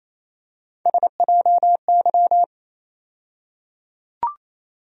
Morse code, heard as a keyed tone, about 28 wpm, 700 Hz. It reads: SJY
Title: USA Call Sign Postfixes - 3 Letters 28wpm